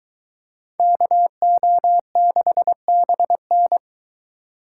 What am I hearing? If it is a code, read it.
Morse code, heard as a keyed tone, about 22 wpm, 700 Hz. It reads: KO6BN